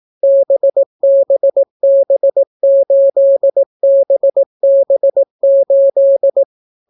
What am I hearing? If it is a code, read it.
Morse code, heard as a keyed tone, about 18 wpm, 550 Hz. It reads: BBB8BB8